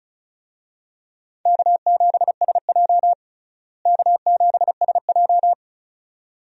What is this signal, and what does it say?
Morse code, heard as a keyed tone, about 36 wpm, 700 Hz. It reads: K7SJ K7SJ
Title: USA Call Signs - 4 Characters - Repeated 2x - 35wpm